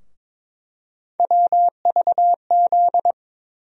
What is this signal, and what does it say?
Morse code, heard as a keyed tone, about 22 wpm, 700 Hz. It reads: WVZ
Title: USA Call Sign Postfixes - 3 Letters 22wpm